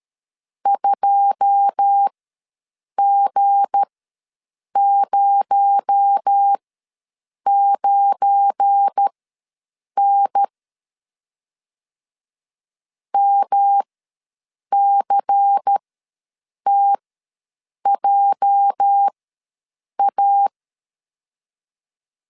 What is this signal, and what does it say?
Morse code, heard as a keyed tone, about 13 wpm, spaced as for 7 wpm, 800 Hz. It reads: 2G09N MCTJA
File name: LEZIONI / 9 2 ws 12wpm 3406